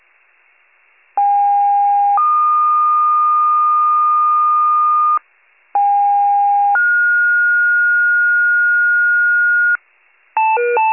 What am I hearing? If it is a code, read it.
Two-tone page, heard as a one-second tone, then a three-second tone, about 810 then 1220 Hz, about 810 then 1430 Hz, with noise. It is heard from 1.2 to 9.8 s.